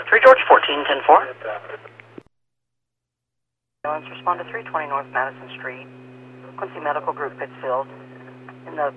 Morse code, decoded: none

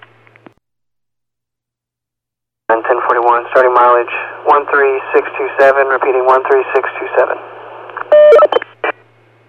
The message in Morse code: none

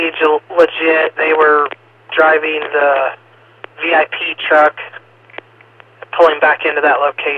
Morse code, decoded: none

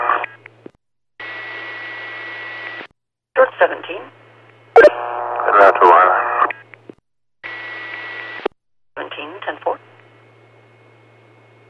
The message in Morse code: none

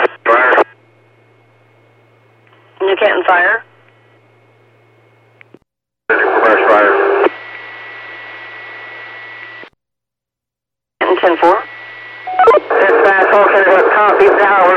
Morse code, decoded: none

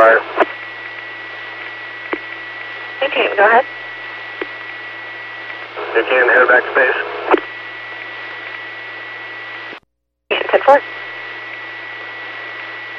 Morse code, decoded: none

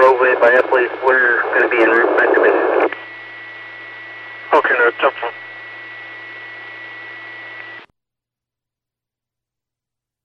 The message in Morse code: none

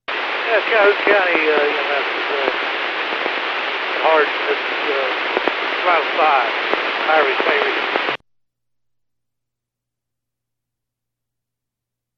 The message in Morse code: none

